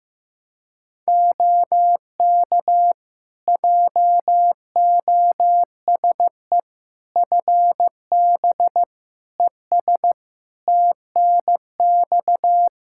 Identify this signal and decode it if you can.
Morse code, heard as a keyed tone, about 15 wpm, 700 Hz. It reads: OK JOSE FB ES TNX
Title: K1USN SST QSO Practice 15wpm